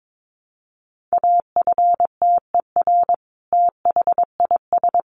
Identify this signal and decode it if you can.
Morse code, heard as a keyed tone, about 22 wpm, 700 Hz. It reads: AFTER THIS
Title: Sentences from Top 1200 Words 22wpm